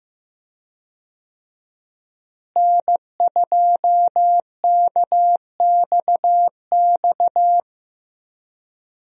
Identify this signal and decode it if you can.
Morse code, heard as a keyed tone, about 15 wpm, 700 Hz. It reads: N2KXX
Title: USA Call Signs - 5 Characters - Repeated 3x - 15wpm